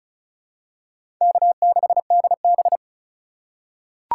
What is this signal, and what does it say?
Morse code, heard as a keyed tone, about 36 wpm, 690 Hz. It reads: K6DB
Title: USA Call Signs - 4 Characters - 2x Character Spacing - 35wpm